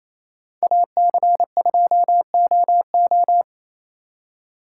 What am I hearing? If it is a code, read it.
Morse code, heard as a keyed tone, about 33 wpm, 700 Hz. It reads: AC2OO
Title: Speed Racing - USA 2x2 Call Signs 28wpm